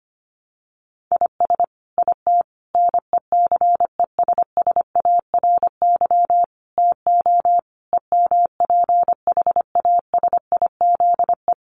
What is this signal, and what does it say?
Morse code, heard as a keyed tone, about 25 wpm, 700 Hz. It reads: IS IT NECESSARY TO EMPHASIZE
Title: Sentences from Top 1500 Words 25wpm